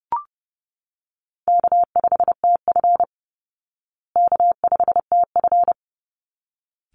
Morse code, heard as a keyed tone, about 30 wpm, 700 Hz. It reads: K5TF K5TF
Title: Easy Call Signs Repeated 2x - 30wpm